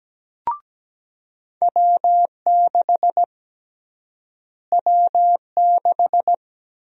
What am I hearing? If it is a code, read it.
Morse code, heard as a keyed tone, about 17 wpm, 700 Hz. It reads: W6 W6
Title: USA Call Sign Prefixes - Repeated 2x - 17wpm